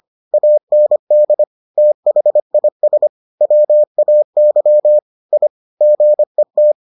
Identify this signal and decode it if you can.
Morse code, heard as a keyed tone, about 25 wpm, 600 Hz. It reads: AND THIS WAY I GET